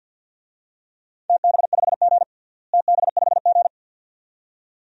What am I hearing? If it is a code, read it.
Morse code, heard as a keyed tone, about 52 wpm, 700 Hz. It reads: TB5G TB5G
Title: DXCC-100 Call Signs - 4 Characters - Repeated 2x - 50wpm